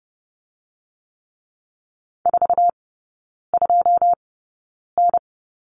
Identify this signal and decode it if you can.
Morse code, heard as a keyed tone, about 30 wpm, spaced as for 9 wpm, 700 Hz. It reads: 42N
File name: 081 Lesson 14e - Number 2 - Three Characters 30wpm